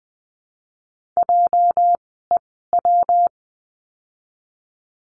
Morse code, heard as a keyed tone, about 19 wpm, spaced as for 14 wpm, 700 Hz. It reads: JEW
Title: 206 Lesson 33f - Letter j - Words 20wpm